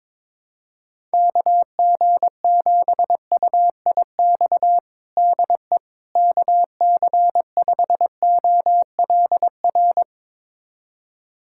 Morse code, heard as a keyed tone, about 21 wpm, 700 Hz. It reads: KG7UIX DE KC5OLR
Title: USA Callsign DE Callsign - 6 Characters - 22wpm